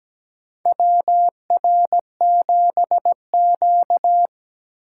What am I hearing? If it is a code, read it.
Morse code, heard as a keyed tone, about 17 wpm, 700 Hz. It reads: WR7Q
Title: USA Call Signs - 4 Characters - 8x Character Spacing - 17wpm